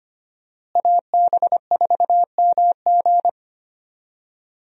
Morse code, heard as a keyed tone, about 25 wpm, 700 Hz. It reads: AB4MG
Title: USA Call Signs - 5 Characters - Repeated 2x - 25wpm